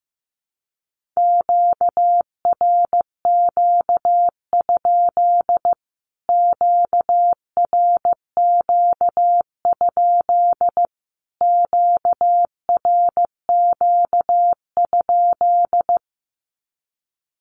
Morse code, heard as a keyed tone, about 15 wpm, 700 Hz. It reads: QRQ? QRQ? QRQ?
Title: Q-Codes Repeated 3x 15wpm